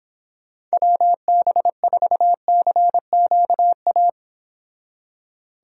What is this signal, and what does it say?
Morse code, heard as a keyed tone, about 26 wpm, 700 Hz. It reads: WB4CQA